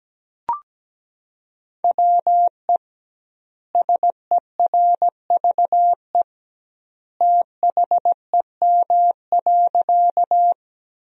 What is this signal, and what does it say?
Morse code, heard as a keyed tone, about 17 wpm, 700 Hz. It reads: WE SERVE THEM.
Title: 2x Word-Spacing - Sentences from Top 400 Words - 17wpm